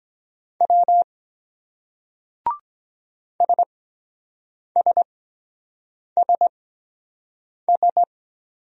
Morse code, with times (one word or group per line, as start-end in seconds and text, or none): 0.6-1.0 W
3.4-8.0 SSSS